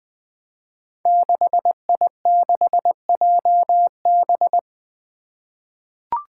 Morse code, heard as a keyed tone, about 20 wpm, 700 Hz. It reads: 6I6JB
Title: Five Challenging Letter-Number Combinations 20wpm